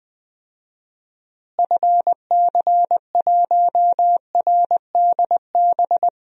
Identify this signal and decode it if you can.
Morse code, heard as a keyed tone, about 20 wpm, 700 Hz. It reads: FC1RDB
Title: DXCC-100 Call Signs - 6 Characters - Repeated 3x - 20wpm